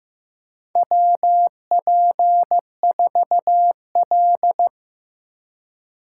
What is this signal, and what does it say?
Morse code, heard as a keyed tone, about 16 wpm, 700 Hz. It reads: WP4L
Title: USA Call Signs - 4 Characters - 2x Character Spacing - 15wpm